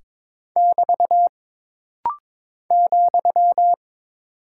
Morse code, heard as a keyed tone, about 22 wpm, 700 Hz. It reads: <BT> ,
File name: practice / 22wpm / Single Punctuation-Prosign - Rapid-Fire - 22wpm